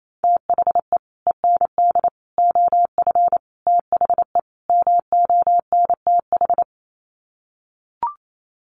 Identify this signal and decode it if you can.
Morse code, heard as a keyed tone, about 28 wpm, 700 Hz. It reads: THE END OF THE MONTH
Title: Sets of 5 Words Encore 28wpm